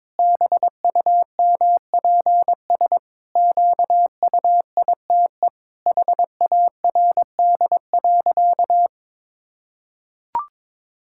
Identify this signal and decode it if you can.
Morse code, heard as a keyed tone, about 22 wpm, 700 Hz. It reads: BUMPS QUITE HARD.
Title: Pangram Challenge v2 - 22wpm